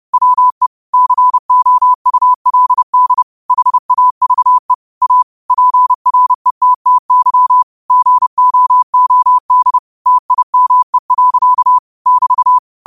Morse code, heard as a keyed tone, about 30 wpm, 1000 Hz. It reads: WE COULD HAVE A PRETTY GOOD TIME. <BT>
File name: qotdfiles / 30wpm